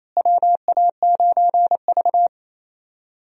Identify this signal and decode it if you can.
Morse code, heard as a keyed tone, about 28 wpm, 700 Hz. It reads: WA9V
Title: USA Call Signs - 4 Characters - Repeated 2x - 28wpm